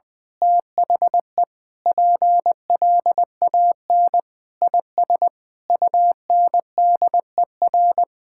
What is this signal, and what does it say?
Morse code, heard as a keyed tone, about 20 wpm, 700 Hz. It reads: THE PLAN IS UNDER
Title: Sentences with 9-Word Phrases - 20wpm